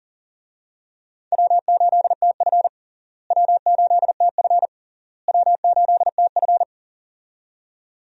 Morse code, heard as a keyed tone, about 40 wpm, 700 Hz. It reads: W8TF W8TF W8TF
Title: USA Call Signs - 4 Characters - Repeated 3x - 40wpm